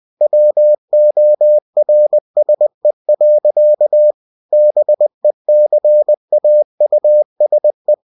Morse code, heard as a keyed tone, about 20 wpm, 600 Hz. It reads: WORSE. BECAUSE
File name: cw-books / winnie-the-pooh / winnie-20wpm-page-0039